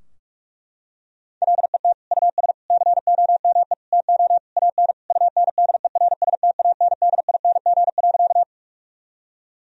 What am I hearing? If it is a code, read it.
Morse code, heard as a keyed tone, about 41 wpm, 700 Hz. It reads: LET US COME TO AN UNDERSTANDING.
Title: Sentences from Top 1000 Words - Review Part 3 - 45wpm